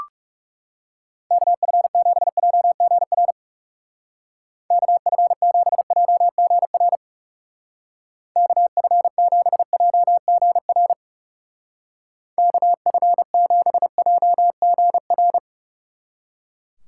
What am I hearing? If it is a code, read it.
Morse code, heard as a keyed tone, about 40 wpm, 700 Hz. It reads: KF7JGR KF7JGR KF7JGR KF7JGR